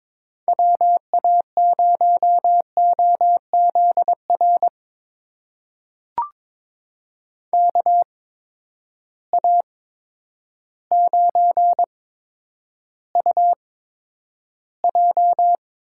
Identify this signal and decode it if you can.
Morse code, heard as a keyed tone, about 22 wpm, spaced as for 10 wpm, 700 Hz. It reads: WA0OZR KA9UJ
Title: USA Call Signs - 6 Characters - 8x Character Spacing - 22wpm